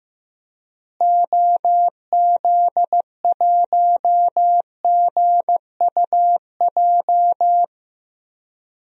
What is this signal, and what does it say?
Morse code, heard as a keyed tone, about 16 wpm, 700 Hz. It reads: OZ1GUJ
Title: DXCC-100 Call Signs - 6 Characters - Repeated 3x - 15wpm